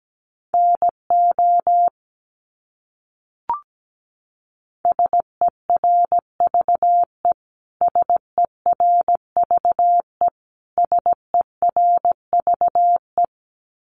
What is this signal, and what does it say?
Morse code, heard as a keyed tone, about 17 wpm, 700 Hz. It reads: NO SERVE SERVE SERVE
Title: Top 500 Words Repeated 17wpm